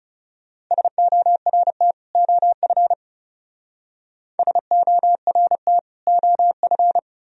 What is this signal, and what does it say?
Morse code, heard as a keyed tone, about 37 wpm, 700 Hz. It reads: SORT OF SORT OF
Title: Speed-Racing - Sets of 2 Words 30wpm